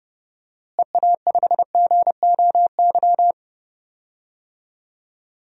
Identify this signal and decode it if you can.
Morse code, heard as a keyed tone, about 30 wpm, 700 Hz. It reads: EA5GOY
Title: DXCC-100 Call Signs - 6 Characters - Repeated 3x - 30wpm